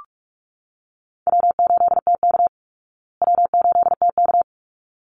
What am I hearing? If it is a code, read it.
Morse code, heard as a keyed tone, about 44 wpm, 700 Hz. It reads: W8TK W8TK
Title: Call Signs Repeated 3x - 45wpm